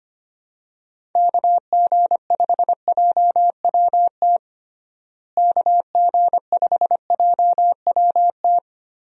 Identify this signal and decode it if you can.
Morse code, heard as a keyed tone, about 25 wpm, 700 Hz. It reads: KG5JWT KG5JWT
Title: USA Call Signs - 6 Characters - Repeated 3x - 25wpm